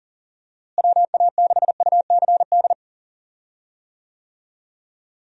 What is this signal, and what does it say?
Morse code, heard as a keyed tone, about 39 wpm, 700 Hz. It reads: WA6UCD
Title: USA Call Signs - 6 Characters - Repeated 3x - 40wpm